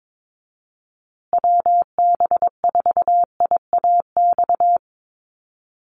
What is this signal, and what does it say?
Morse code, heard as a keyed tone, about 22 wpm, 700 Hz. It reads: WB4IAX
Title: USA Call Signs Repeated 3x 22wpm